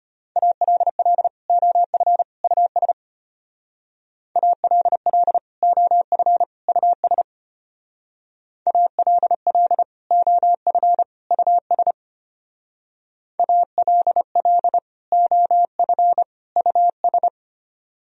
Morse code, heard as a keyed tone, about 33 wpm, 700 Hz. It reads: ALL OF US ALL OF US ALL OF US ALL OF US